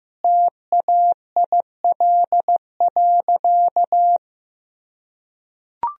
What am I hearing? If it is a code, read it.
Morse code, heard as a keyed tone, about 15 wpm, 700 Hz. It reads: TAIL.
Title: Sentences from Top 1000 Words - Review Part 3 - 15wpm